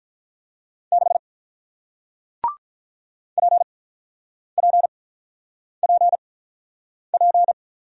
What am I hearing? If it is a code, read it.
Morse code, heard as a keyed tone, about 46 wpm, spaced as for 9 wpm, 700 Hz. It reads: 6 PPPP